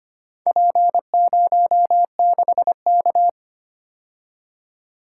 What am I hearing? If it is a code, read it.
Morse code, heard as a keyed tone, about 25 wpm, 700 Hz. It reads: P06K
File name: practice / 25wpm / Four Letter-Number Combinations - 25wpm